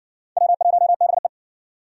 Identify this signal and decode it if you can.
Morse code, heard as a keyed tone, about 56 wpm, 700 Hz. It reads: W1BE